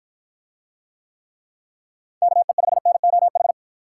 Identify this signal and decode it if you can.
Morse code, heard as a keyed tone, about 49 wpm, 700 Hz. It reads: KE5NOH